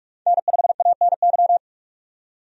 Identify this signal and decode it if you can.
Morse code, heard as a keyed tone, about 44 wpm, 700 Hz. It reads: N5ANY
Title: USA Call Signs - 5 Characters - 45wpm